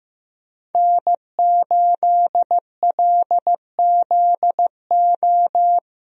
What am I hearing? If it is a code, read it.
Morse code, heard as a keyed tone, about 16 wpm, 700 Hz. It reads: N8LZO
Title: USA Call Signs - 5 Characters - Repeated 3x - 15wpm